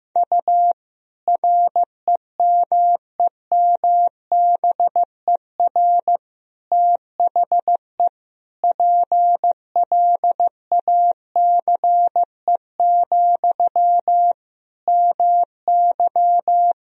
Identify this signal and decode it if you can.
Morse code, heard as a keyed tone, about 15 wpm, 700 Hz. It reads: U REMEMBER THE PLACE, MY
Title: Sentences from Top 400 Words and Abbreviations 15wpm